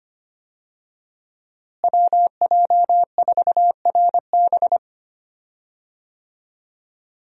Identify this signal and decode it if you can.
Morse code, heard as a keyed tone, about 25 wpm, 700 Hz. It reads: WJ4RB
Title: USA Call Signs - 5 Characters - Repeated 3x - 25wpm